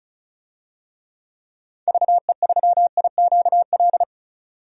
Morse code, heard as a keyed tone, about 34 wpm, 700 Hz. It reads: VE3IQL
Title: DXCC-100 Call Signs - 6 Characters - 35wpm